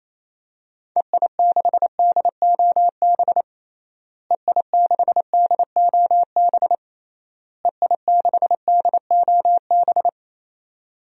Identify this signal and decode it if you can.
Morse code, heard as a keyed tone, about 28 wpm, 700 Hz. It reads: EI6DOB EI6DOB EI6DOB
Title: DXCC-100 Call Signs - 6 Characters - Repeated 3x - 28wpm